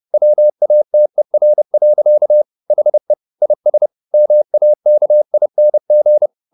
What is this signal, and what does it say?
Morse code, heard as a keyed tone, about 30 wpm, 600 Hz. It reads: WATER. HE IS MAKING